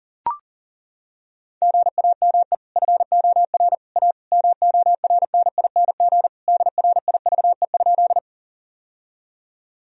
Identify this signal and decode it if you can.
Morse code, heard as a keyed tone, about 40 wpm, 700 Hz. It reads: GAME FOR A MORNING DRIVE?
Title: Sentences from Top 1000 Words - Review Part 3 - 40wpm